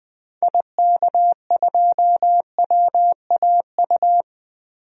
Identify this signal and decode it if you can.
Morse code, heard as a keyed tone, about 20 wpm, 700 Hz. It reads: IK2WAU
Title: DXCC-100 Call Signs - 6 Characters - 20wpm